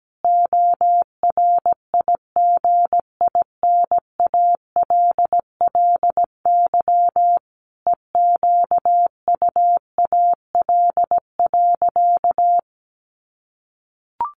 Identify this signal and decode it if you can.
Morse code, heard as a keyed tone, about 17 wpm, 700 Hz. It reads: ORIGINALLY EQUAL.
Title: Sentences from Top 1500 Words 17wpm